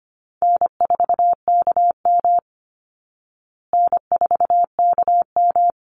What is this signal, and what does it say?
Morse code, heard as a keyed tone, about 25 wpm, 700 Hz. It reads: N4KM N4KM
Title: Easy Call Signs Repeated 2x - 25wpm